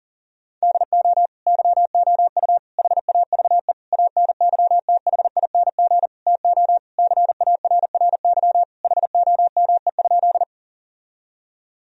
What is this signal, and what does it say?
Morse code, heard as a keyed tone, about 40 wpm, 700 Hz. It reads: DO YOU HAVE ANYTHING TO CARRY HOME?
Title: Sentences from Top 1000 Words - Review Part 1 - 40wpm